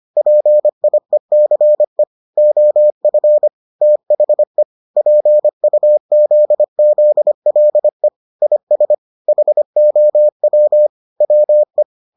Morse code, heard as a keyed tone, about 25 wpm, 600 Hz. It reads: PIECE OF THE PUZZLE IS HOW WE